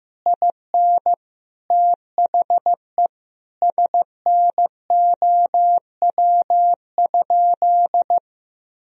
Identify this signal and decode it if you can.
Morse code, heard as a keyed tone, about 16 wpm, 700 Hz. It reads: IN THE SNOW?
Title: Sentences from Top 1000 Words - Review Part 3 - 15wpm